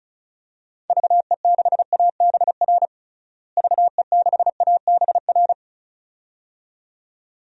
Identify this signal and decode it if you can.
Morse code, heard as a keyed tone, about 34 wpm, 700 Hz. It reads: VE6ABR VE6ABR
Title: DXCC-100 Call Signs - 6 Characters - Repeated 2x - 35wpm